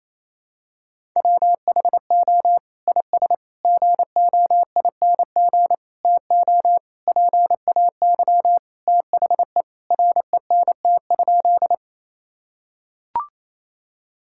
Morse code, heard as a keyed tone, about 28 wpm, 700 Hz. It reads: WHO IS GOING TO PAY THE RENT?
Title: Sentences from Top 1000 Words - Review Part 5 - 28wpm